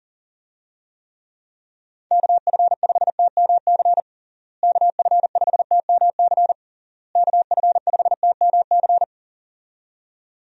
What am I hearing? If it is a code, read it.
Morse code, heard as a keyed tone, about 40 wpm, 700 Hz. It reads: KF5TMC KF5TMC KF5TMC
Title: USA Call Signs - 6 Characters - Repeated 3x - 40wpm